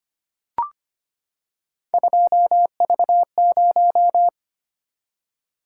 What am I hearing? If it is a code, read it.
Morse code, heard as a keyed tone, about 25 wpm, 700 Hz. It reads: 2V0
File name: Three Letter-Number Combinations - 25wpm